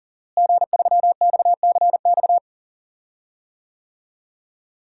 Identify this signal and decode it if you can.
Morse code, heard as a keyed tone, about 39 wpm, 700 Hz. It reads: G3XCX
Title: DXCC-100 Call Signs - 5 Characters - Repeated 3x - 40wpm